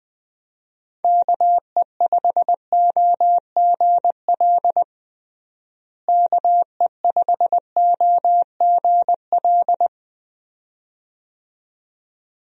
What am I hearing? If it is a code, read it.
Morse code, heard as a keyed tone, about 20 wpm, 700 Hz. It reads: KE5OGL KE5OGL
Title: USA Call Signs - 6 Characters - Repeated 2x - 20wpm